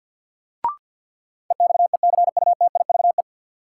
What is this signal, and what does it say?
Morse code, heard as a keyed tone, about 51 wpm, 700 Hz. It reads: EXECUTIVE
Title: Sets of 5 Words Encore - New Word Drill - 50wpm